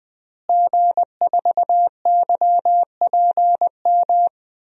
Morse code, heard as a keyed tone, about 20 wpm, 700 Hz. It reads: G4YPM
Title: DXCC-100 Call Signs - 5 Characters - Repeated 2x - 20wpm